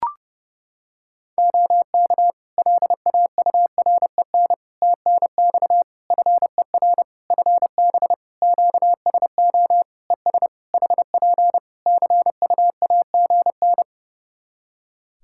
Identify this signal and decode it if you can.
Morse code, heard as a keyed tone, about 31 wpm, 700 Hz. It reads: OK LAUREN TNX FER FB QSO ES HP CUAGN
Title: Practice QSO - Exchange 4 - Ending - 30wpm